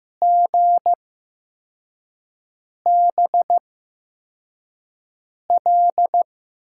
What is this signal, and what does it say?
Morse code, heard as a keyed tone, about 15 wpm, spaced as for 4 wpm, 700 Hz. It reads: GBL
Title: USA Call Signs - 6 Characters - 8x Character Spacing - 15wpm